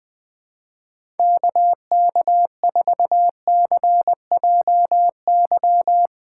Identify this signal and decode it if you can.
Morse code, heard as a keyed tone, about 20 wpm, 700 Hz. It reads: KK4CJY
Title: Speed Racing - USA Call Signs 17wpm